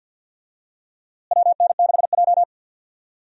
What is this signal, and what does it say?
Morse code, heard as a keyed tone, about 49 wpm, 700 Hz. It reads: WN6J
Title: USA Call Signs - 4 Characters - Repeated 2x - 50wpm